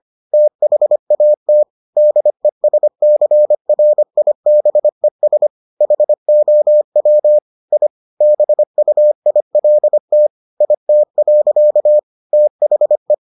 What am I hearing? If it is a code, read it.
Morse code, heard as a keyed tone, about 25 wpm, 600 Hz. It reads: THAT DESCRIBES HOW I BUILT IT. THE